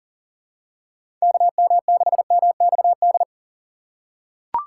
Morse code, heard as a keyed tone, about 40 wpm, 700 Hz. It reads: KM6MXD
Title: USA Call Signs - 6 Characters - 3x Character Spacing - 40wpm